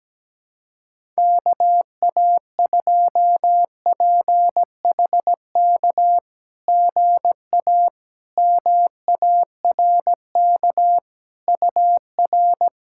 Morse code, heard as a keyed tone, about 17 wpm, 690 Hz. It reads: KA2PHK GA MARK UR
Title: SOTA QSO Practice 17wpm